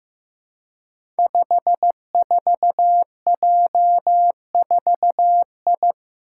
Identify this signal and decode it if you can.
Morse code, heard as a keyed tone, about 15 wpm, 700 Hz. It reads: 54J4I
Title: Five Challenging Letter-Number Combinations 15wpm